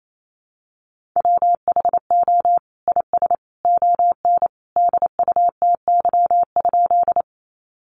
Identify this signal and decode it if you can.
Morse code, heard as a keyed tone, about 28 wpm, 700 Hz. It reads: WHO IS ON DUTY?
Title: Sentences from Top 1500 Words 28wpm